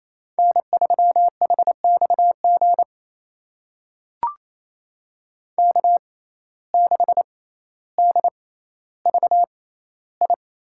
Morse code, heard as a keyed tone, about 28 wpm, spaced as for 14 wpm, 700 Hz. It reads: N3HXG K6DVI
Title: USA Call Signs - 6x Character Spacing - 28wpm